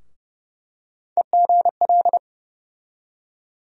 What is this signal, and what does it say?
Morse code, heard as a keyed tone, about 31 wpm, 700 Hz. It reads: EGL
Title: USA Call Sign Postfixes - 3 Letters 30wpm